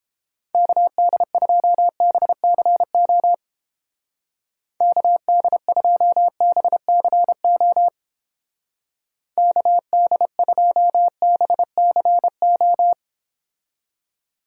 Morse code, heard as a keyed tone, about 29 wpm, 700 Hz. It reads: KD2BCO KD2BCO KD2BCO